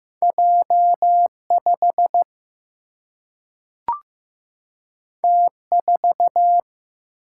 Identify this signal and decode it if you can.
Morse code, heard as a keyed tone, about 15 wpm, 700 Hz. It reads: J 5 T 4
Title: Single-Letter Single-Number 15wpm